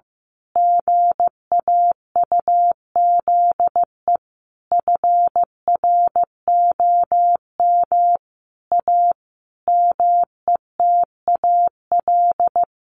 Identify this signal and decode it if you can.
Morse code, heard as a keyed tone, about 14 wpm, 700 Hz. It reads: GAUZE FROM A METAL
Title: A Sound Of Thunder 15wpm